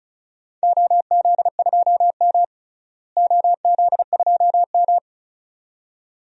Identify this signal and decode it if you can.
Morse code, heard as a keyed tone, about 34 wpm, 700 Hz. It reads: OZ2M OZ2M
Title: DXCC-100 Call Signs - 4 Characters - Repeated 2x - 35wpm